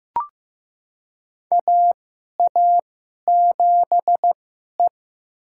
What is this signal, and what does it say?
Morse code, heard as a keyed tone, about 15 wpm, spaced as for 11 wpm, 700 Hz. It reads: AA7E
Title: USA Call Signs - 4 Characters - 2x Character Spacing - 15wpm